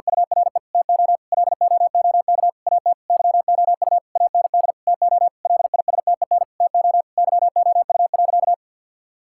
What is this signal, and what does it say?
Morse code, heard as a keyed tone, about 50 wpm, 700 Hz. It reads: URE TO LOOK AT YOU AND TO LISTEN TO YOU.
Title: Sentences from Top 1600 Words 50wpm